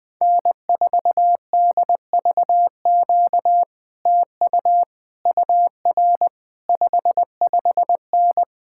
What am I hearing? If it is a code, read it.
Morse code, heard as a keyed tone, about 20 wpm, 700 Hz. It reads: N4DVQ TU UR 55N
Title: Speed-Racing Common QSO Phrases - 17wpm